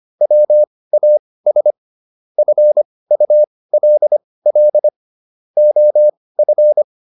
Morse code, heard as a keyed tone, about 25 wpm, spaced as for 18 wpm, 600 Hz. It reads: WAS FULL OF